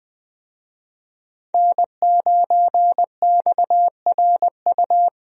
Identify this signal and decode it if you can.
Morse code, heard as a keyed tone, about 20 wpm, 700 Hz. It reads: N9XRU
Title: USA Call Signs - 5 Characters - Repeated 3x - 20wpm